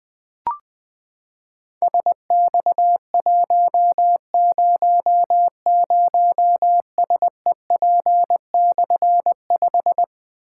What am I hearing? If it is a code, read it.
Morse code, heard as a keyed tone, about 20 wpm, 700 Hz. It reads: SX100SEP/5
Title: Difficult Call Signs 20wpm